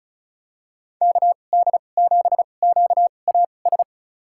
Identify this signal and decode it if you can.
Morse code, heard as a keyed tone, about 36 wpm, spaced as for 25 wpm, 700 Hz. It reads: KD7QAS
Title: USA Call Signs - 6 Characters - 2x Character Spacing - 35wpm